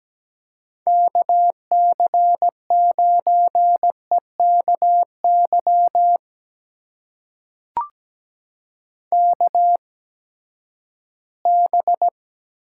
Morse code, heard as a keyed tone, about 17 wpm, spaced as for 12 wpm, 700 Hz. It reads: KC9EKY KB
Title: USA Call Signs - 6 Characters - 8x Character Spacing - 17wpm